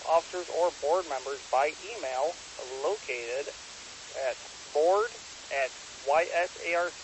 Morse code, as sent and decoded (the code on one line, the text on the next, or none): none